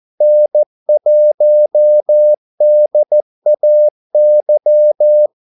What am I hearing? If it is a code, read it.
Morse code, heard as a keyed tone, about 14 wpm, 600 Hz. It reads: N1DAY
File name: Callsigns-14wpm